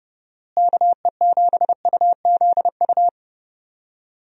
Morse code, heard as a keyed tone, about 30 wpm, 700 Hz. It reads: KE7UZU
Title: USA Call Signs - 6 Characters - Repeated 3x - 30wpm